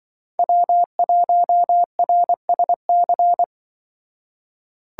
Morse code, heard as a keyed tone, about 24 wpm, 700 Hz. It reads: W1RSC
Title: Speed Racing - USA 1x3 Call Signs 20wpm